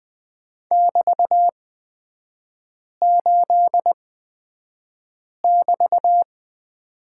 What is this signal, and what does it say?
Morse code, heard as a keyed tone, about 20 wpm, spaced as for 5 wpm, 700 Hz. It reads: <BT>8<BT>
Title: 224 Lesson 36e - BT - Three Characters 20wpm